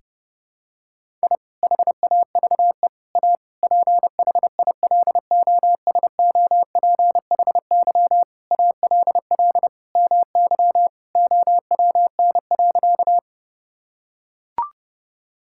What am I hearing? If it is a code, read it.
Morse code, heard as a keyed tone, about 32 wpm, 700 Hz. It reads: I HAVE A PHILOSOPHY ALL MY OWN.
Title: Sentences from Top 1000 Words - Review Part 6 - 30wpm